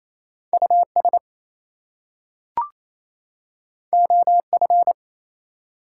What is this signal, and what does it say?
Morse code, heard as a keyed tone, about 29 wpm, 700 Hz. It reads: US OF
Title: Two Letter Words - 28wpm